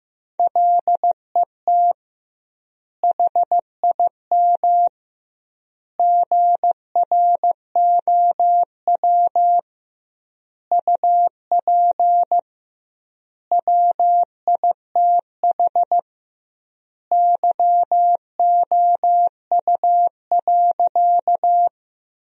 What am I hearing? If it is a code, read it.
Morse code, heard as a keyed tone, about 15 wpm, 700 Hz. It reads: LET HIM GROW UP WITH YOU.
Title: 2x Word-Spacing - Sentences from Top 300 Words - 15wpm